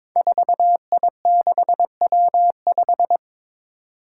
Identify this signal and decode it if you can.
Morse code, heard as a keyed tone, about 22 wpm, 700 Hz. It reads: 4I6W5
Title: Five Challenging Letter-Number Combinations 22wpm